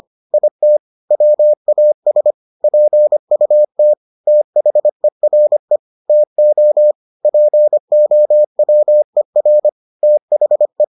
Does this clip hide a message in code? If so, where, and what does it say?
Yes: Morse code from 0.3 to 10.8 s, heard as a keyed tone, about 25 wpm, 600 Hz. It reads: IT WAS PUT THERE TO POWER THE